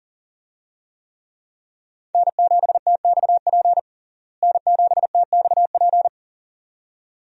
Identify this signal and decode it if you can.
Morse code, heard as a keyed tone, about 40 wpm, 700 Hz. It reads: N7TXP N7TXP
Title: USA Call Signs - 5 Characters - Repeated 2x - 40wpm